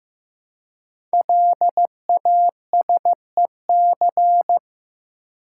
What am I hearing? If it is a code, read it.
Morse code, heard as a keyed tone, about 15 wpm, 700 Hz. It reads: LASEC
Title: Code Groups - 5 Letters - 15wpm